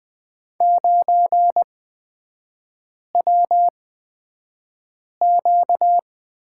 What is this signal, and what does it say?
Morse code, heard as a keyed tone, about 20 wpm, spaced as for 5 wpm, 700 Hz. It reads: 9WQ
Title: 255 Lesson 41e - Period - Three Characters 20wpm